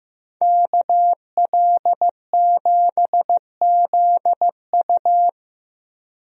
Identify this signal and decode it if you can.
Morse code, heard as a keyed tone, about 16 wpm, 700 Hz. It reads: KL7ZU
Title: USA Call Signs - 5 Characters - Repeated 2x - 15wpm